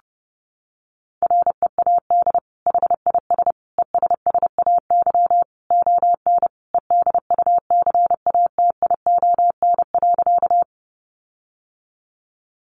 Morse code, heard as a keyed tone, about 30 wpm, 700 Hz. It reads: READ HIS ESSAY ON EDUCATION.
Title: Sentences from Top 1500 Words 30wpm